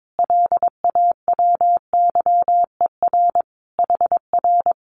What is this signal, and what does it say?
Morse code, heard as a keyed tone, about 22 wpm, 700 Hz. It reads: LAWYER HR
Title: Common QSO Phrases Encore 22wpm